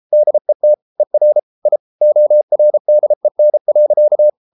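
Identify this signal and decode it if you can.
Morse code, heard as a keyed tone, about 33 wpm, 600 Hz. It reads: DET ER I ORDEN.
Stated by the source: Morse code at 33 wpm, QRQ.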